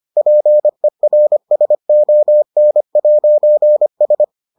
Morse code, heard as a keyed tone, about 25 wpm, 600 Hz. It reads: PERSON'S